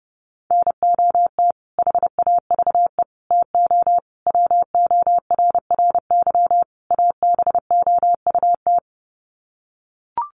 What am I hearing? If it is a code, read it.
Morse code, heard as a keyed tone, about 30 wpm, 700 Hz. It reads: NOT HAVE TO WORRY ABOUT
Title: Sets of 5 Words 30wpm